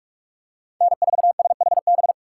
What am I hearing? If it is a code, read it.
Morse code, heard as a keyed tone, about 41 wpm, 700 Hz. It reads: N4SHB
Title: USA Call Signs - 3x Character Spacing - 45wpm